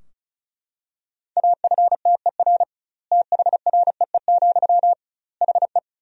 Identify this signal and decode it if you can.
Morse code, heard as a keyed tone, about 36 wpm, 700 Hz. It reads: AFTER THREE, HE
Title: 2x Word-Spacing - Sentences from Top 200 Words - 35wpm